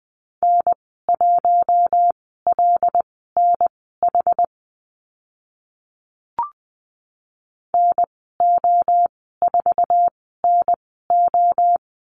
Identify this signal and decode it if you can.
Morse code, heard as a keyed tone, about 20 wpm, spaced as for 15 wpm, 700 Hz. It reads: N1LNH NO4NO
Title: 082 Lesson 14g - Number 2 - Call Signs 20wpm